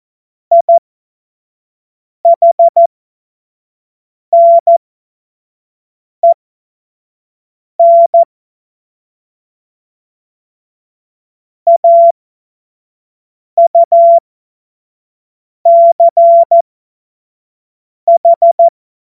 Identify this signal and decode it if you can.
Morse code, heard as a keyed tone, about 14 wpm, spaced as for 5 wpm, 680 Hz. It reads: IHNEN AUCH
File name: Witz-09-September-2025-25-F70